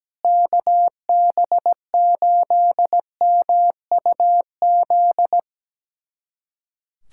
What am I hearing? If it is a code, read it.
Morse code, heard as a keyed tone, about 17 wpm, 700 Hz. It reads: KB8MUZ
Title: USA Callsign DE Callsign 17wpm